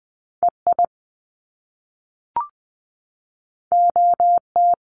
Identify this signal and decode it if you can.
Morse code, heard as a keyed tone, about 20 wpm, 700 Hz. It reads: EI OT
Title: Top Two Letter Combinations in Words 20wpm